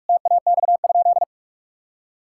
Morse code, heard as a keyed tone, about 46 wpm, 700 Hz. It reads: TAX?
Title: Sentences from Top 1000 Words - Review Part 5 - 45wpm